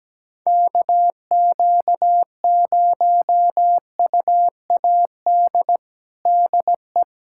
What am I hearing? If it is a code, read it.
Morse code, heard as a keyed tone, about 17 wpm, 700 Hz. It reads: KQ0UAD DE
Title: USA Callsign DE Callsign 17wpm